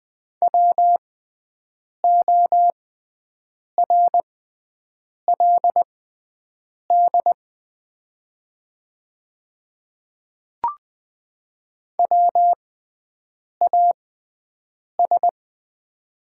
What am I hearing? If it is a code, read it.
Morse code, heard as a keyed tone, about 20 wpm, spaced as for 7 wpm, 700 Hz. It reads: WORLD WAS
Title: 6x Character Spacing - Top 200 Words - 20wpm